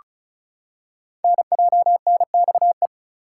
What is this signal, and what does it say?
Morse code, heard as a keyed tone, about 34 wpm, 700 Hz. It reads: NJNXE
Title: Code Groups - 5 Letters - 35wpm